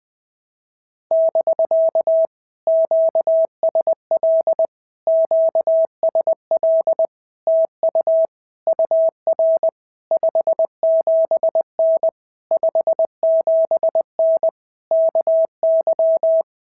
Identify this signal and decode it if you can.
Morse code, heard as a keyed tone, about 20 wpm, 650 Hz. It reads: <BK> QSL QSL TU UR 57N 57N KY
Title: POTA QSO Practice - 20wpm